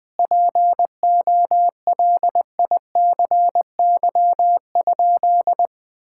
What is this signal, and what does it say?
Morse code, heard as a keyed tone, about 21 wpm, 700 Hz. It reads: POLICY?
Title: Sentences from Top 1000 Words - Review Part 5 - 20wpm